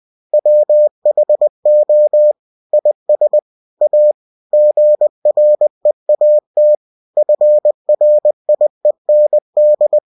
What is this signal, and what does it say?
Morse code, heard as a keyed tone, about 20 wpm, 600 Hz. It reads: WHO IS A GREAT FRIEND
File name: cw-books / winnie-the-pooh / winnie-20wpm-page-0014